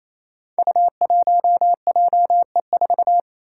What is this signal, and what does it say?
Morse code, heard as a keyed tone, about 28 wpm, 700 Hz. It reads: U1JE4
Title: Five Challenging Letter-Number Combinations 28wpm